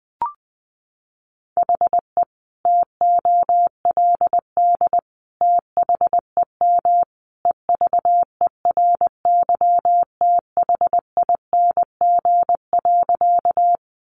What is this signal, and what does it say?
Morse code, heard as a keyed tone, about 20 wpm, 700 Hz. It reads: HE TOLD THEM EVERYTHING.
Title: Sentences from Top 700 Words 20wpm